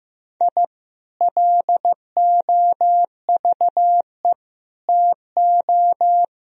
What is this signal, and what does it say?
Morse code, heard as a keyed tone, about 15 wpm, 700 Hz. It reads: I LOVE TO
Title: Sentences from Top 1000 Words - Review Part 3 - 15wpm